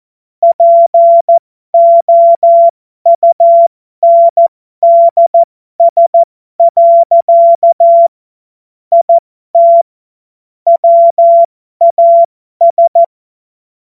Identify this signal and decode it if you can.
Morse code, heard as a keyed tone, about 14 wpm, spaced as for 12 wpm, 680 Hz. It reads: POUNDS. IT WAS
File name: Morsekurs / Witze